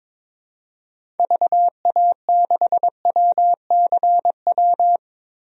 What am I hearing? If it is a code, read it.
Morse code, heard as a keyed tone, about 22 wpm, 700 Hz. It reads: VA6WCW
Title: DXCC-100 Call Signs - 6 Characters - 22wpm